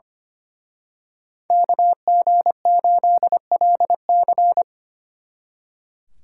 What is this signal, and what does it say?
Morse code, heard as a keyed tone, about 29 wpm, 700 Hz. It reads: KG8LC
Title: Speed Racing - USA 2x2 Call Signs 25wpm